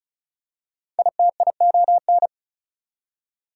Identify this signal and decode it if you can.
Morse code, heard as a keyed tone, about 36 wpm, 700 Hz. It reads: ITION
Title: Prefixes and Suffixes 35wpm